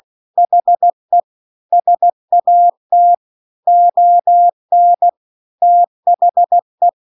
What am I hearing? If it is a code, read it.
Morse code, heard as a keyed tone, about 16 wpm, 700 Hz. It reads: HE SAT ON THE
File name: CW-The-Big-Twoheaded-River-2-16WPM